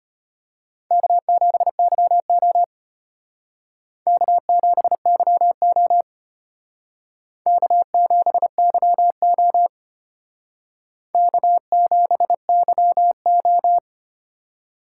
Morse code, heard as a keyed tone, about 32 wpm, 700 Hz. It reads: K7YO K7YO K7YO K7YO